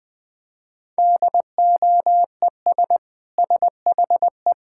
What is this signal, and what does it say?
Morse code, heard as a keyed tone, about 21 wpm, 700 Hz. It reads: DOES SHE
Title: Sentences from Top 1000 Words - Review Part 6 - 20wpm